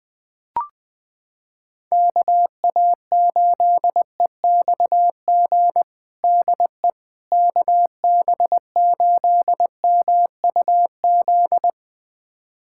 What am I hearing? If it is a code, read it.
Morse code, heard as a keyed tone, about 21 wpm, 700 Hz. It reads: KA8EXG DE KB8MUZ
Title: USA Callsign DE Callsign 20wpm